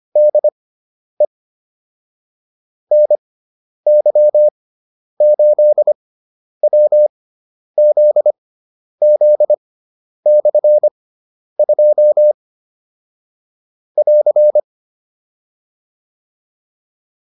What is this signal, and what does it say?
Morse code, heard as a keyed tone, about 25 wpm, spaced as for 10 wpm, 600 Hz. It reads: DE NY8WZZ/2 <AR>
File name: QSO301 10